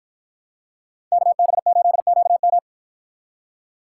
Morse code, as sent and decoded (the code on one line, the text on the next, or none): -.- -... ---.. --.- --
KB8QM